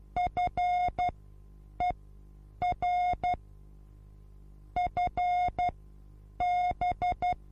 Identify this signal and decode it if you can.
Morse code, heard as a keyed tone, about 12 wpm, spaced as for 8 wpm, 720 Hz. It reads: FER FB
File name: Onlinelehrgang / morsen / DJ4UF Morsen23